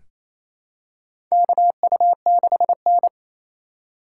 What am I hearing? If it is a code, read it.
Morse code, heard as a keyed tone, about 28 wpm, 700 Hz. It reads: KU6N
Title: USA Call Signs - 4 Characters - 3x Character Spacing - 28wpm